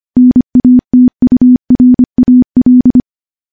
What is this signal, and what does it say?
Morse code, heard as a keyed tone, about 25 wpm, 250 Hz. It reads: NATURAL
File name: morse-training-25WPM